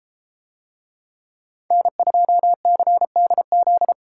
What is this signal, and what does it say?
Morse code, heard as a keyed tone, about 33 wpm, 700 Hz. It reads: N2CDZ